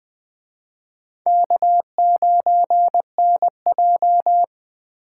Morse code, heard as a keyed tone, about 25 wpm, 700 Hz. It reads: K9NJ